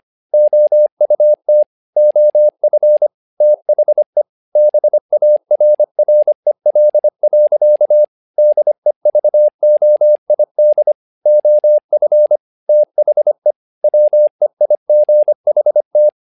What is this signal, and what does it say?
Morse code, heard as a keyed tone, about 25 wpm, 600 Hz. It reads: OUT OF THE BARREL. DEVOID OF THE WEIGHT
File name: learning center / tower trouble / tower trouble 25wpm